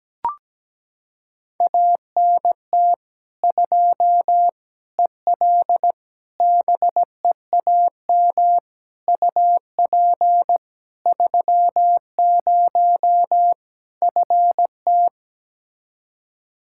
Morse code, heard as a keyed tone, about 17 wpm, 700 Hz. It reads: ANT 2 EL BEAM UP 30 FT
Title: Practice QSO - Exchange 2 - Info - 17wpm